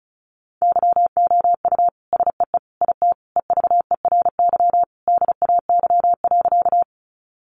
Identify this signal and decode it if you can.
Morse code, heard as a keyed tone, about 34 wpm, 700 Hz. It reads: YOU SEE IT EVERY DAY.
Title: Sentences from Top 200 Words 35wpm